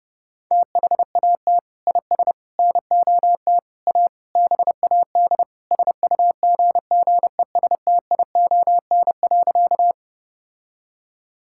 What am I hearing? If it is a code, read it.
Morse code, heard as a keyed tone, about 30 wpm, 700 Hz. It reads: THAT IS NOT A BAD SUGGESTION.
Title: Sentences from Top 1600 Words 30wpm